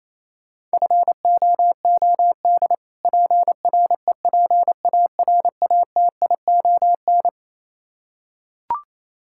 Morse code, heard as a keyed tone, about 28 wpm, 700 Hz. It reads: FOOD PREPARATION
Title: K1USN SST QSO Practice 28wpm